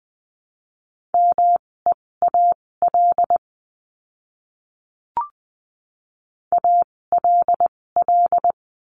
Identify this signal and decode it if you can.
Morse code, heard as a keyed tone, about 20 wpm, spaced as for 16 wpm, 700 Hz. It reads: MEAL ALL
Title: 256 Lesson 42a - Words 12-20wpm